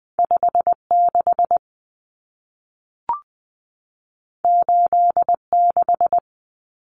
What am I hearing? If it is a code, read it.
Morse code, heard as a keyed tone, about 20 wpm, 700 Hz. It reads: 56 86
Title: Common Cut Numbers - 2-Digit 20wpm